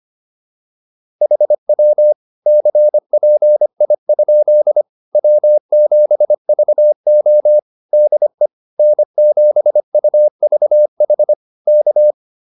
Morse code, heard as a keyed tone, about 25 wpm, 600 Hz. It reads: HW CPI? W7VO DE N7UVH K
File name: QSO105 25